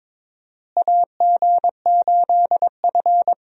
Morse code, heard as a keyed tone, about 21 wpm, 700 Hz. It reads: AG8F
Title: USA Call Signs - 4 Characters - Repeated 3x - 22wpm